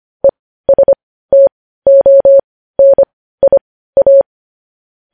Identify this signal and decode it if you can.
Morse code, heard as a keyed tone, about 25 wpm, spaced as for 15 wpm, 550 Hz. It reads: ESTONIA
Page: News Headlines in Morse Code at 15 WPM